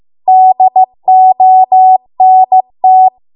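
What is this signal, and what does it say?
Morse code, heard as a keyed tone, about 15 wpm, 750 Hz. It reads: DONT
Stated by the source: Morse code practice at 15 wpm from the national amateur radio association's code archive.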